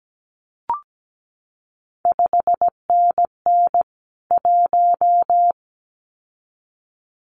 Morse code, heard as a keyed tone, about 17 wpm, 700 Hz. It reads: 5NN 1
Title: CQ WW DX Contest Exchange Encore - 17wpm